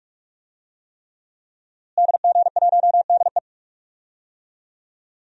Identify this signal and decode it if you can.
Morse code, heard as a keyed tone, about 43 wpm, 700 Hz. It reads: DG1DE
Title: DXCC-100 Call Signs - 5 Characters - Repeated 3x - 45wpm